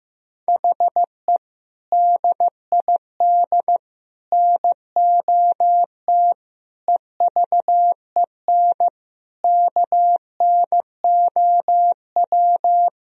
Morse code, heard as a keyed tone, about 15 wpm, 700 Hz. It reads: HE DID NOT EVEN KNOW
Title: Sentences from Top 1000 Words - Review Part 1 - 15wpm